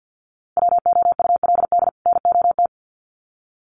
Morse code, heard as a keyed tone, about 50 wpm, 700 Hz. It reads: WOULD NOT
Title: Sets of 2 Words 50wpm